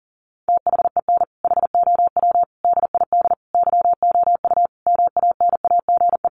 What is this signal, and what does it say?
Morse code, heard as a keyed tone, about 37 wpm, 700 Hz. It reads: THEN HOW DID YOU MANAGE
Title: Sentences from Top 1400 Words 40wpm